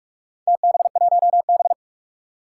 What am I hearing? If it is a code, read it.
Morse code, heard as a keyed tone, about 42 wpm, 700 Hz. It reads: TB1B
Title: Four Challenging Letter-Number Combinations 45wpm